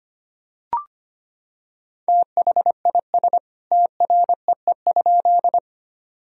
Morse code, heard as a keyed tone, about 25 wpm, 700 Hz. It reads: THIS TREE?
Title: Sentences from Top 1000 Words - Review Part 1 - 25wpm